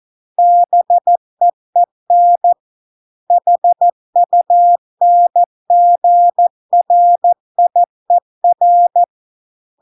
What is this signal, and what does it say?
Morse code, heard as a keyed tone, about 14 wpm, 700 Hz. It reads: BEEN HUNGRIER